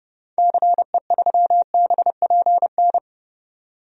Morse code, heard as a keyed tone, about 32 wpm, 700 Hz. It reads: CE3BPN
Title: DXCC-100 Call Signs - 6 Characters - 30wpm